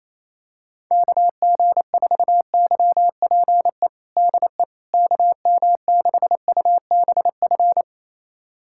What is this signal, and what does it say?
Morse code, heard as a keyed tone, about 27 wpm, 700 Hz. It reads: KG4YPE DE KM6UBF
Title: USA Callsign DE Callsign 28wpm